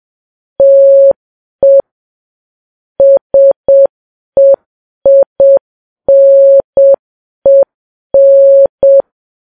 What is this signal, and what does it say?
Morse code, heard as a keyed tone, about 7 wpm, 550 Hz. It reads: TE SEINEN
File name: Files Web tls / Morse on9fv / morse7kps